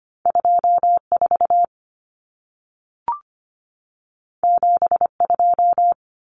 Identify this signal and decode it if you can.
Morse code, heard as a keyed tone, about 25 wpm, 700 Hz. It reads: 24 72
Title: Numbers - 2-Digit 25wpm